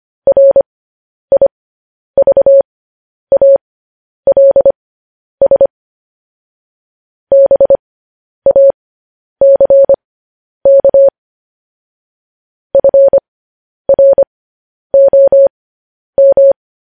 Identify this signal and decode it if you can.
Morse code, heard as a keyed tone, about 25 wpm, spaced as for 10 wpm, 550 Hz. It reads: RIVALS BACK FROM
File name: podcast / News headlines-1x-25x10